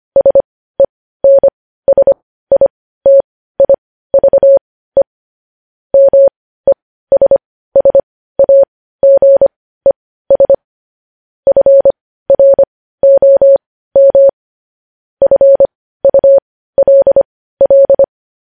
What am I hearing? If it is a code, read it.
Morse code, heard as a keyed tone, about 25 wpm, spaced as for 15 wpm, 550 Hz. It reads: SENSITIVE MESSAGES FROM FULL
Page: News Headlines in Morse Code at 15 WPM